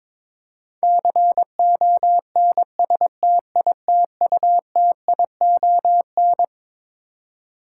Morse code, heard as a keyed tone, about 22 wpm, 700 Hz. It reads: CONSTITUTION
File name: Twelve Letter Words - 22wpm